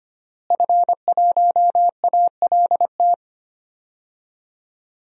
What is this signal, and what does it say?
Morse code, heard as a keyed tone, about 25 wpm, 700 Hz. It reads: F1ALT
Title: DXCC-100 Call Signs - 5 Characters - Repeated 3x - 25wpm